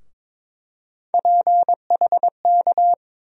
Morse code, heard as a keyed tone, about 22 wpm, 700 Hz. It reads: PHK
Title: USA Call Sign Postfixes - 3 Letters 22wpm